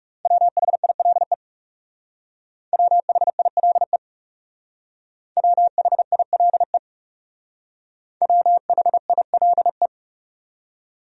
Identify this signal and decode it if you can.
Morse code, heard as a keyed tone, about 37 wpm, 700 Hz. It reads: WHILE WHILE WHILE WHILE
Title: Speed-Racing Top 500 Words 30wpm